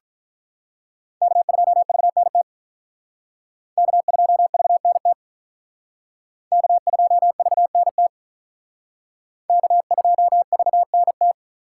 Morse code, heard as a keyed tone, about 44 wpm, 700 Hz. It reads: K2VNT K2VNT K2VNT K2VNT